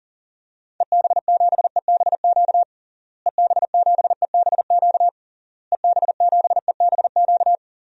Morse code, heard as a keyed tone, about 40 wpm, 700 Hz. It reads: EB7EBQ EB7EBQ EB7EBQ
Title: DXCC-100 Call Signs - 6 Characters - Repeated 3x - 40wpm